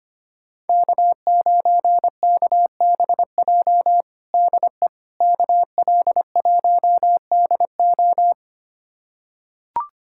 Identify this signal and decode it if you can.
Morse code, heard as a keyed tone, about 25 wpm, 700 Hz. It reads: K9KBJ DE KL1DO
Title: USA Callsign DE Callsign - 5 Characters - 25wpm